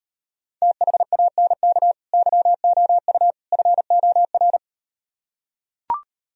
Morse code, heard as a keyed tone, about 38 wpm, 700 Hz. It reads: THANK YOU FOR